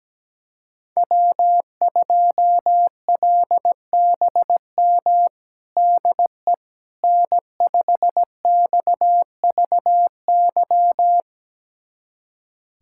Morse code, heard as a keyed tone, about 17 wpm, 700 Hz. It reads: W2LBM DE N5XVY
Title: USA Callsign DE Callsign 17wpm